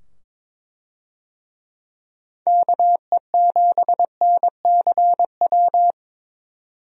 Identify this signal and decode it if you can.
Morse code, heard as a keyed tone, about 21 wpm, 700 Hz. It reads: KE7NCW
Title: USA Call Signs - 6 Characters - 22wpm